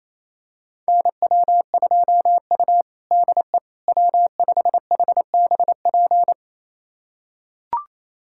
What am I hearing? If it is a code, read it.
Morse code, heard as a keyed tone, about 28 wpm, 700 Hz. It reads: NW2U DE W5HBP
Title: USA Callsign DE Callsign 28wpm